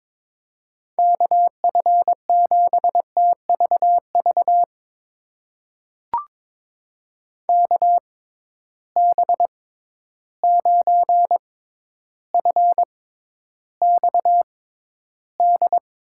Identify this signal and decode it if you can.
Morse code, heard as a keyed tone, about 22 wpm, spaced as for 11 wpm, 700 Hz. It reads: KF7TVV KB9FXD
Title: USA Call Signs - 6 Characters - 6x Character Spacing - 22wpm